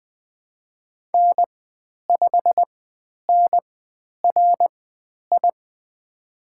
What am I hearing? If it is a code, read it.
Morse code, heard as a keyed tone, about 20 wpm, spaced as for 10 wpm, 700 Hz. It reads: N5NRI
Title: 095 Lesson 16g - Letter c - Call Signs 20wpm